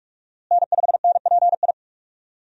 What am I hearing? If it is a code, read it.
Morse code, heard as a keyed tone, about 46 wpm, 700 Hz. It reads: N5NPI
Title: USA Call Signs - 5 Characters - Repeated 2x - 45wpm